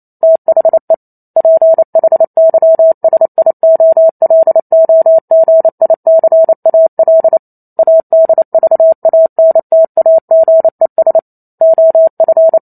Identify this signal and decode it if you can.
Morse code, heard as a keyed tone, about 28 wpm, 650 Hz. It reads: THE PHYSIOLOGICAL ADVANTAGES OF